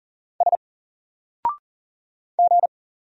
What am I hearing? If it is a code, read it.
Morse code, heard as a keyed tone, about 40 wpm, 700 Hz. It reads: S G
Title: Single Letter-Number Alt 40wpm